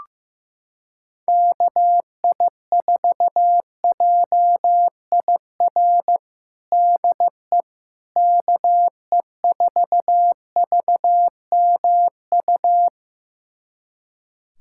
Morse code, heard as a keyed tone, about 15 wpm, 700 Hz. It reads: KI4JIR DE KE4VMU
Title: USA Callsign DE Callsign - 6 Characters - 15wpm